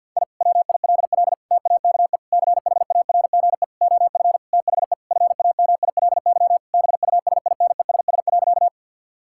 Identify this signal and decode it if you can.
Morse code, heard as a keyed tone, about 50 wpm, 700 Hz. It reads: I WILL TAKE CHARGE OF THE FAMILY BUSINESS.
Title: Sentences from Top 1000 Words - Review Part 6 - 50wpm